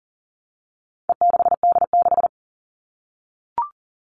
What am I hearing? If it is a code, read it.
Morse code, heard as a keyed tone, about 40 wpm, 700 Hz. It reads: E6D6
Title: All Cut Numbers - 4-Digit 40wpm